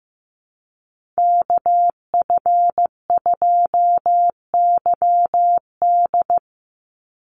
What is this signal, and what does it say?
Morse code, heard as a keyed tone, about 15 wpm, 700 Hz. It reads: KF2YD
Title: USA Call Signs 15wpm